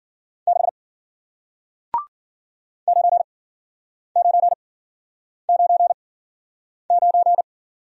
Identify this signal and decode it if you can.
Morse code, heard as a keyed tone, about 53 wpm, spaced as for 9 wpm, 700 Hz. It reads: 6 9999